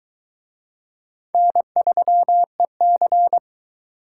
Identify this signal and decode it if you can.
Morse code, heard as a keyed tone, about 22 wpm, 700 Hz. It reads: N3EC